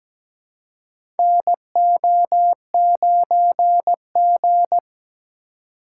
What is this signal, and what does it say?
Morse code, heard as a keyed tone, about 17 wpm, 700 Hz. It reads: NO9G
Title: USA Call Signs - 4 Characters - 4x Character Spacing - 17wpm